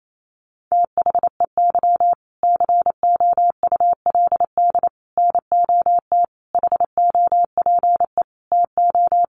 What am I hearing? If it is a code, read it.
Morse code, heard as a keyed tone, about 28 wpm, 700 Hz. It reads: THEY COULD NOT HOPE TO
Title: Sentences from Top 1100 Words 28wpm